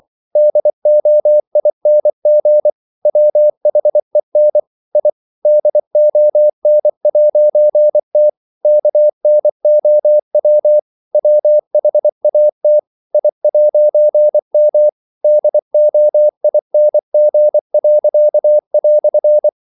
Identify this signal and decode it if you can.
Morse code, heard as a keyed tone, about 24 wpm, 600 Hz. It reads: DOING WHEN I DON'T KNOW WHAT I'M DOING."
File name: CW / Practice / EnglishQuotations0424